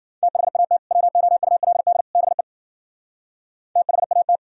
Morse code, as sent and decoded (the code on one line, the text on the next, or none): - .... .- - / .-- --- ..- .-.. -.. / -... / - .... .- -
THAT WOULD BE THAT